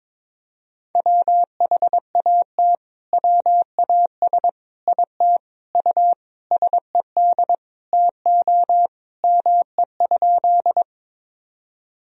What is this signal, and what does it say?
Morse code, heard as a keyed tone, about 22 wpm, 700 Hz. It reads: WHAT WAS IT U SED TO ME?
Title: Sentences from Top 200 Words and Abbreviations 22wpm